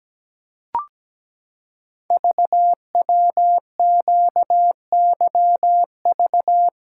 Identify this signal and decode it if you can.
Morse code, heard as a keyed tone, about 17 wpm, 700 Hz. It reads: VWQYV
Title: Code Groups - 5 Letters - 17wpm